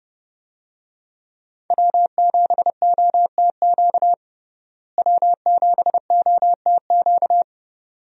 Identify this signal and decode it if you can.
Morse code, heard as a keyed tone, about 30 wpm, 700 Hz. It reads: W7OTQ W7OTQ
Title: USA Call Signs - 5 Characters - Repeated 2x - 30wpm